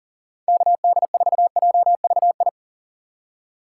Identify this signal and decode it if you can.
Morse code, heard as a keyed tone, about 40 wpm, 700 Hz. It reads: KD4JVI
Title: USA Call Signs - 6 Characters - 3x Character Spacing - 40wpm